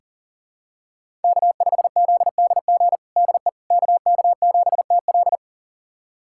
Morse code, heard as a keyed tone, about 40 wpm, 700 Hz. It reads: K5ZDG DE KK7TL
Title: USA Callsign DE Callsign - 5 Characters - 40wpm